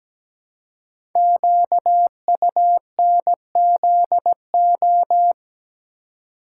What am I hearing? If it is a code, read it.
Morse code, heard as a keyed tone, about 17 wpm, 700 Hz. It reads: QUNZO
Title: Code Groups - 5 Letters - 17wpm